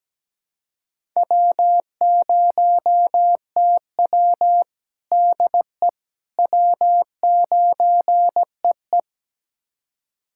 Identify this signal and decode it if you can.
Morse code, heard as a keyed tone, about 17 wpm, 700 Hz. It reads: W0TW DE W9EE
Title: USA Callsign DE Callsign - 4 Characters - 17wpm